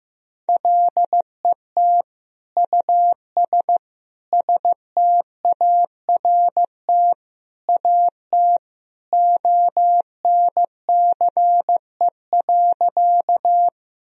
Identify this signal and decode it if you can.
Morse code, heard as a keyed tone, about 15 wpm, 700 Hz. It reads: LET US START AT ONCE.
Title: Sentences from Top 1000 Words - Review Part 1 - 15wpm